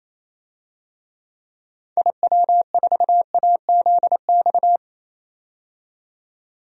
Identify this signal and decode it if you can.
Morse code, heard as a keyed tone, about 27 wpm, 700 Hz. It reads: IW4AZX
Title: DXCC-100 Call Signs - 6 Characters - 28wpm